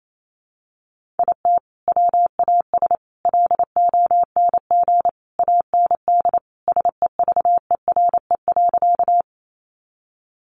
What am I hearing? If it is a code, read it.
Morse code, heard as a keyed tone, about 28 wpm, 700 Hz. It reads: IT WAS LONG AND SEVERE.
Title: Sentences from Top 1400 Words 28wpm